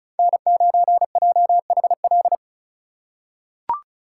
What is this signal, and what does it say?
Morse code, heard as a keyed tone, about 36 wpm, 700 Hz. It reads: N9JHL
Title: K1USN SST QSO Practice - 35wpm